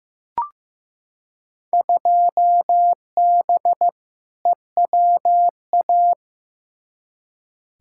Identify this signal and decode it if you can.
Morse code, heard as a keyed tone, about 15 wpm, 700 Hz. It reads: 2B EWA
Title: Field Day Exchanges - 15wpm